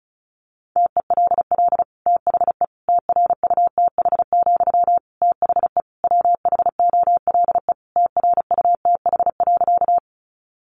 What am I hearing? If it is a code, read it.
Morse code, heard as a keyed tone, about 34 wpm, 700 Hz. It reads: TELL THE TRUTH, THE WHOLE TRUTH.
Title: Sentences from Top 800 Words 35wpm